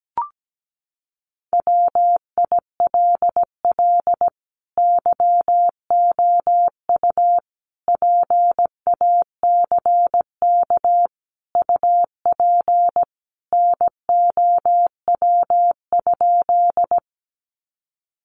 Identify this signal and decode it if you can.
Morse code, heard as a keyed tone, about 17 wpm, 700 Hz. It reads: WILL YOU PACK UP NOW?
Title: Sentences from Top 1300 Words 17wpm